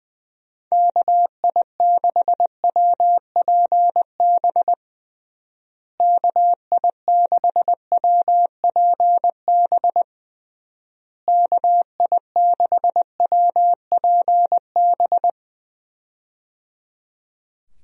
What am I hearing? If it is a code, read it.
Morse code, heard as a keyed tone, about 20 wpm, 700 Hz. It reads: KI6WPB KI6WPB KI6WPB
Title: USA Call Signs - 6 Characters - Repeated 3x - 20wpm